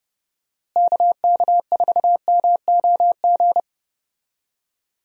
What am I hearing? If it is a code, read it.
Morse code, heard as a keyed tone, about 31 wpm, 700 Hz. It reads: KK4MOG
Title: USA Call Signs - 6 Characters - Repeated 3x - 30wpm